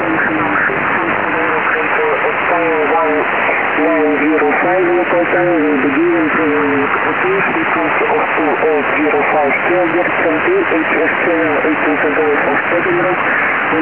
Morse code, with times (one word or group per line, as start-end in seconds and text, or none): none